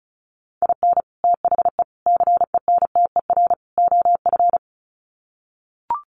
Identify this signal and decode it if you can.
Morse code, heard as a keyed tone, about 34 wpm, 700 Hz. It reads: IN THE CENTER OF
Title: Sets of 4 Words 35wpm